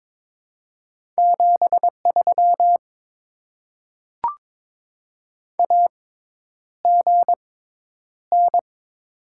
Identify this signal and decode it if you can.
Morse code, heard as a keyed tone, about 22 wpm, spaced as for 10 wpm, 690 Hz. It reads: 73 AGN
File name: Top 20 QSO Elements - 6x Character Spacing - 22wpm